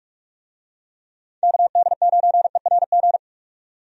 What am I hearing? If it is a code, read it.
Morse code, heard as a keyed tone, about 47 wpm, 700 Hz. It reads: KD9ERG
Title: USA Call Signs - 6 Characters - 45wpm